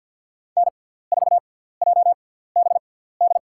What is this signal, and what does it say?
Morse code, heard as a keyed tone, about 51 wpm, spaced as for 17 wpm, 700 Hz. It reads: N4JBD
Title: USA Call Signs - 5 Characters - 6x Character Spacing - 50wpm